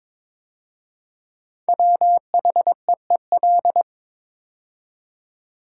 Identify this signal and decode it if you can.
Morse code, heard as a keyed tone, about 22 wpm, 690 Hz. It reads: WHEEL
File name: 5 Letter Words - First 3 Letters Given - 22wpm